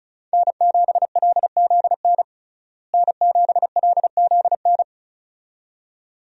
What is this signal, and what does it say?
Morse code, heard as a keyed tone, about 36 wpm, 700 Hz. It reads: N7LZN N7LZN
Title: USA Call Signs - 5 Characters - Repeated 2x - 35wpm